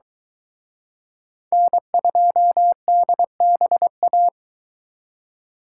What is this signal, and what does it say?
Morse code, heard as a keyed tone, about 23 wpm, 700 Hz. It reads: N2DBA